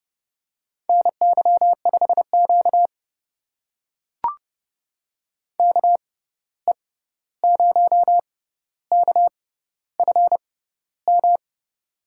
Morse code, heard as a keyed tone, about 30 wpm, spaced as for 14 wpm, 700 Hz. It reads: NY5Q KE0KFM
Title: USA Call Signs - 6x Character Spacing - 30wpm